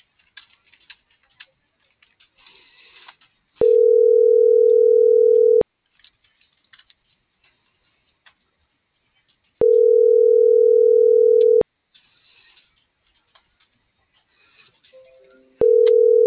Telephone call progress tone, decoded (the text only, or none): ringback tone